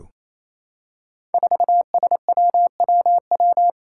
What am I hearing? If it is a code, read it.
Morse code, heard as a keyed tone, about 29 wpm, 700 Hz. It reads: 4SWWW
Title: Five Challenging Letter-Number Combinations 28wpm